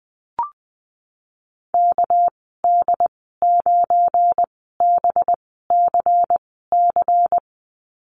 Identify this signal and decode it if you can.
Morse code, heard as a keyed tone, about 20 wpm, spaced as for 15 wpm, 700 Hz. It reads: KD9BCC
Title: 207 Lesson 33g - Letter j - Call Signs 20wpm